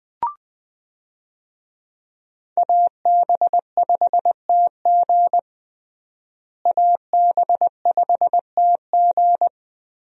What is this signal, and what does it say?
Morse code, heard as a keyed tone, about 20 wpm, 700 Hz. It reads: AB5TG AB5TG
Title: USA Call Signs - 5 Characters - Repeated 3x - 20wpm